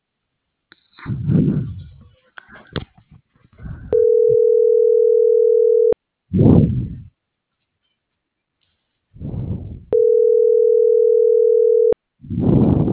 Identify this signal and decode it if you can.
Ringback tone (North America), 440 + 480 Hz.